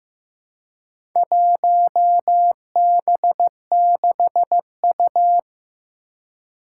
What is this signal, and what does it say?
Morse code, heard as a keyed tone, about 15 wpm, 700 Hz. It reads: 1B6U
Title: Four Challenging Letter-Number Combinations 15wpm